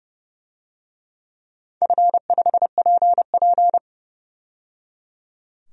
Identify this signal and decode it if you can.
Morse code, heard as a keyed tone, about 28 wpm, 700 Hz. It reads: F5PP
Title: DXCC-100 Call Signs - 4 Characters - 30wpm